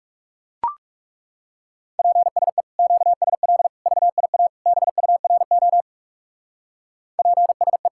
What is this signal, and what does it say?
Morse code, heard as a keyed tone, about 38 wpm, 700 Hz. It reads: PSE QSL VIA BURO PSE